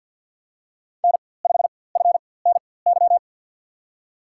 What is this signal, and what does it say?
Morse code, heard as a keyed tone, about 47 wpm, spaced as for 23 wpm, 700 Hz. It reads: N5FNY
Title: USA Call Signs - 5 Characters - 4x Character Spacing - 50wpm